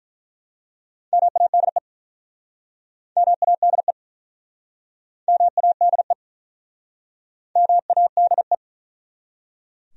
Morse code, heard as a keyed tone, about 47 wpm, 700 Hz. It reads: MADE MADE MADE MADE